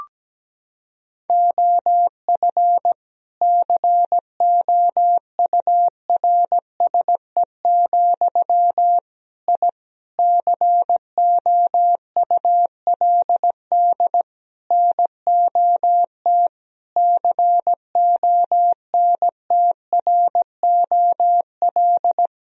Morse code, heard as a keyed tone, about 17 wpm, 700 Hz. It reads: OF COURSE, I COULD NOT CONTROL
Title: Sentences from Top 1000 Words - Review Part 3 - 17wpm